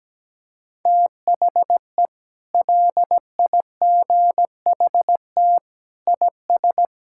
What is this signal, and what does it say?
Morse code, heard as a keyed tone, about 17 wpm, 700 Hz. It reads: THE LIGHT IS
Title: Sentences from Top 1000 Words - Review Part 6 - 17wpm